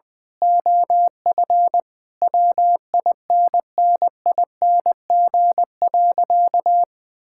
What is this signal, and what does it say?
Morse code, heard as a keyed tone, about 20 wpm, 700 Hz. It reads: OF WINNING.
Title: Sentences with 8-Word Phrases - Spoken Beginning - 20wpm